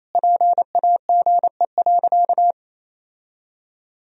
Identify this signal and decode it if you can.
Morse code, heard as a keyed tone, about 28 wpm, 700 Hz. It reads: PAGE.
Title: Sentences from Top 1000 Words - Review Part 1 - 28wpm